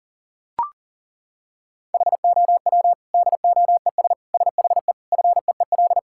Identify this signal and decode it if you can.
Morse code, heard as a keyed tone, about 37 wpm, 700 Hz. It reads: HOW DOES SHE FEEL
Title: Sentences from Top 1000 Words - Review Part 6 - 40wpm